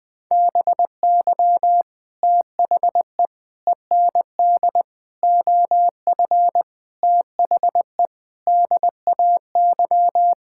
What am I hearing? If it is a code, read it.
Morse code, heard as a keyed tone, about 20 wpm, 700 Hz. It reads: BY THE END OF THE DAY
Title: Sentences with 4-Word Phrases - Spoken Beginning - 20wpm